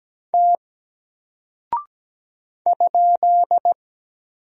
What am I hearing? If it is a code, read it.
Morse code, heard as a keyed tone, about 17 wpm, 700 Hz. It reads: T ?
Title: Single Character (Letter:Number:Punctuation:Prosign) Alt 17wpm